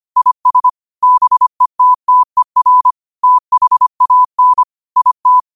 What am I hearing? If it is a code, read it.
Morse code, heard as a keyed tone, about 25 wpm, 1000 Hz. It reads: IS BETTER THAN IT
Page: Quotes for Sat, 16 Aug 2025 in Morse Code at 25 words per minute.